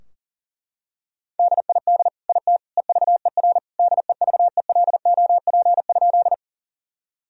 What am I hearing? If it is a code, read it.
Morse code, heard as a keyed tone, about 40 wpm, 700 Hz. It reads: DID IT EVER DEVELOP?
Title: Sentences from Top 1000 Words - Review Part 2 - 40wpm